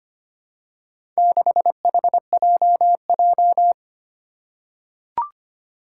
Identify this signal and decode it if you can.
Morse code, heard as a keyed tone, about 26 wpm, 700 Hz. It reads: 6HJJ
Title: Four Challenging Letter-Number Combinations 25wpm